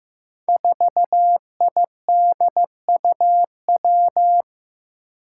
Morse code, heard as a keyed tone, about 15 wpm, 700 Hz. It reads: 4IDUW
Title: Five Challenging Letter-Number Combinations 15wpm